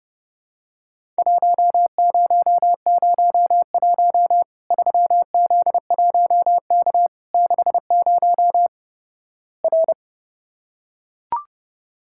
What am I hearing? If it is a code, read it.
Morse code, heard as a keyed tone, about 30 wpm, 700 Hz. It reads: 1001 3Z1K 60 R
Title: Worked All Europe DX Contest - QTC Exchange - 30wpm